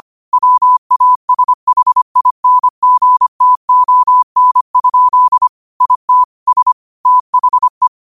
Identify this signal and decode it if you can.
Morse code, heard as a keyed tone, about 25 wpm, 1000 Hz. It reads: WASHINGTON? IT S THE
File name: qotdfiles / 25wpm